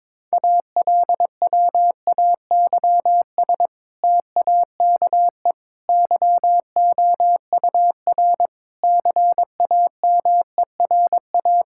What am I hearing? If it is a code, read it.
Morse code, heard as a keyed tone, about 21 wpm, 700 Hz. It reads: ALWAYS TAKE YOUR CAMERA
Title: Sentences from Top 1000 Words - Review Part 5 - 22wpm